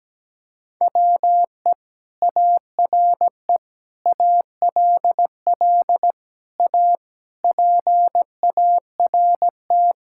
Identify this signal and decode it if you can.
Morse code, heard as a keyed tone, about 17 wpm, 700 Hz. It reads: WE ARE ALL A PART
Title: Sentences from Top 1000 Words - Review Part 6 - 17wpm